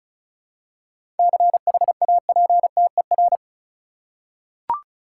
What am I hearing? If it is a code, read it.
Morse code, heard as a keyed tone, about 36 wpm, 700 Hz. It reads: CHAPTER
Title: Seven Letter Words - 35wpm